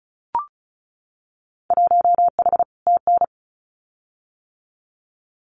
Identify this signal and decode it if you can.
Morse code, heard as a keyed tone, about 34 wpm, 700 Hz. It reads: 1H TN
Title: Winter Field Day Exchanges 35wpm